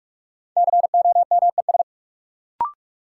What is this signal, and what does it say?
Morse code, heard as a keyed tone, about 43 wpm, 700 Hz. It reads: COMES
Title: Sentences from Top 1000 Words - Review Part 5 - New Word Drill - 45wpm